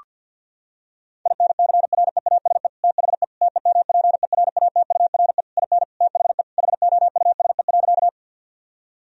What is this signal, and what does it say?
Morse code, heard as a keyed tone, about 52 wpm, 700 Hz. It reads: INCREASE THE TEMPERATURE IN THE HOUSE.
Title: Sentences from Top 1000 Words - Review Part 4 - 50wpm